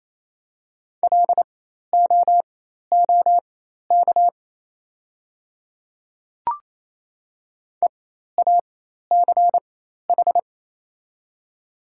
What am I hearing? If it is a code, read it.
Morse code, heard as a keyed tone, about 28 wpm, spaced as for 13 wpm, 700 Hz. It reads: LOOK EACH
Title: 4x Character Spacing - Top 100 Words - 28wpm